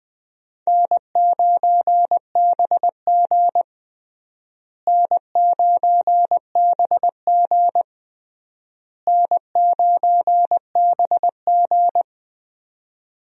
Morse code, heard as a keyed tone, about 20 wpm, 700 Hz. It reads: N9BG N9BG N9BG
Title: USA Call Signs - 4 Characters - Repeated 3x - 20wpm